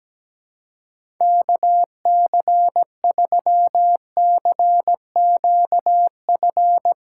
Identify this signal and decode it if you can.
Morse code, heard as a keyed tone, about 17 wpm, 700 Hz. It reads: KC3CQF
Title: USA Callsign DE Callsign - 6 Characters - 17wpm